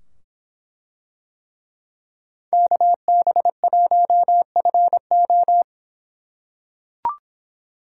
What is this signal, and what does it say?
Morse code, heard as a keyed tone, about 24 wpm, 700 Hz. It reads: KB1FO